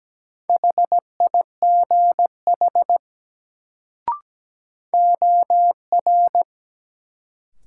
Morse code, heard as a keyed tone, about 17 wpm, 700 Hz. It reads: HIGH OR
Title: Sets of 2 Words - New Word Drill - 17wpm